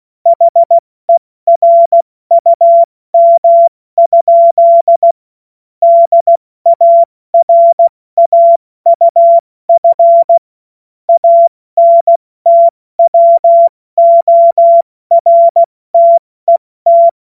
Morse code, heard as a keyed tone, about 16 wpm, 680 Hz. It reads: HERUM? DARAUF ANTWORTET
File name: Morsekurs / Witze / Witz 27 Februar 2024 70 F80